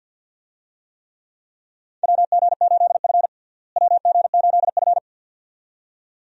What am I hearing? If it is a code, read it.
Morse code, heard as a keyed tone, about 46 wpm, 700 Hz. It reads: WG8F WG8F
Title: USA Call Signs - 4 Characters - Repeated 2x - 50wpm